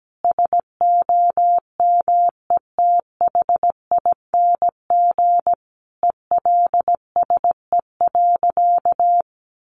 Morse code, heard as a keyed tone, about 17 wpm, 700 Hz. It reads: SOMETHING ELSE.
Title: Sentences from Top 800 Words 17wpm